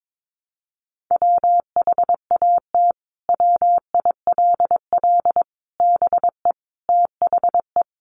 Morse code, heard as a keyed tone, about 22 wpm, 700 Hz. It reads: WHAT WILL BE THE
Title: Sentences from Top 700 Words 22wpm